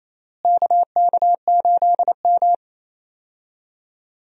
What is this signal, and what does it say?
Morse code, heard as a keyed tone, about 27 wpm, 700 Hz. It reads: KK8M
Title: USA Call Signs - 4 Characters - Repeated 3x - 28wpm